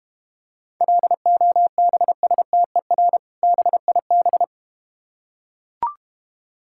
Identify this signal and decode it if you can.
Morse code, heard as a keyed tone, about 32 wpm, 700 Hz. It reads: LOBSTER BIB